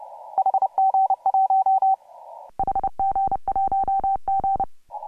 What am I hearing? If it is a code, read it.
Morse code, heard as a keyed tone, about 30 wpm, 750 Hz. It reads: HG1 HG1G
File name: AUDIO2018 / HG1G CW